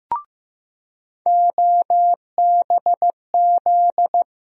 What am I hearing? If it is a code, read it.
Morse code, heard as a keyed tone, about 16 wpm, 700 Hz. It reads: OBZ
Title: USA Call Sign Postfixes - 3 Letters 15wpm